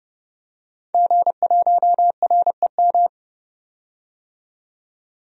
Morse code, heard as a keyed tone, about 32 wpm, 700 Hz. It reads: G1REM
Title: DXCC-100 Call Signs - 5 Characters - 30wpm